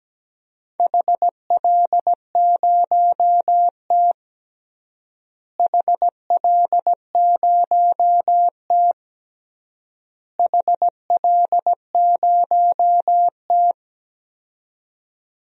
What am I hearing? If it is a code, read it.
Morse code, heard as a keyed tone, about 17 wpm, 700 Hz. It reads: HL0T HL0T HL0T
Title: DXCC-100 Call Signs - 4 Characters - Repeated 3x - 17wpm